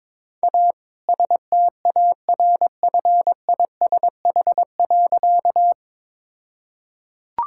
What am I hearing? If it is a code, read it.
Morse code, heard as a keyed tone, about 22 wpm, 690 Hz. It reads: A STARFISH.
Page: morse-code-ninja-practice-22wpm